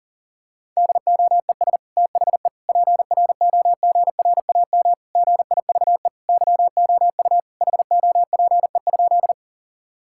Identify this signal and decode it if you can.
Morse code, heard as a keyed tone, about 40 wpm, 700 Hz. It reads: DOES THE PROGRAM GIVE YOU HOPE?
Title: Sentences from Top 1000 Words - Review Part 5 - 40wpm